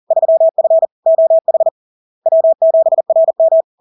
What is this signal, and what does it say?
Morse code, heard as a keyed tone, about 40 wpm, 650 Hz. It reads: 3F OH W7RM
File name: LARK-FD-2022-Log-in-CW-40wpm-650-Hz